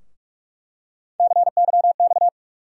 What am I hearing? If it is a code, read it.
Morse code, heard as a keyed tone, about 43 wpm, 700 Hz. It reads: CYX
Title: USA Call Sign Postfixes - 3 Letters 45wpm